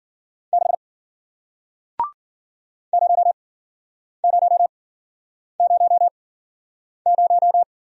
Morse code, heard as a keyed tone, about 53 wpm, spaced as for 9 wpm, 700 Hz. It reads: 6 0000